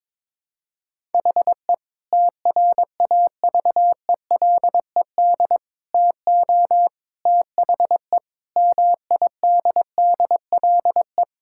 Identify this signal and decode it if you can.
Morse code, heard as a keyed tone, about 22 wpm, 700 Hz. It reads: HE TRAVELED TO THE MIDDLE
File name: practice / 22wpm / Sentences with 3-Word Phrases - 22wpm